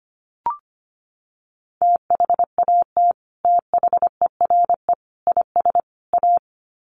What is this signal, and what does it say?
Morse code, heard as a keyed tone, about 25 wpm, 700 Hz. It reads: THAT THERE IS A
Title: Sets of 4 Words 25wpm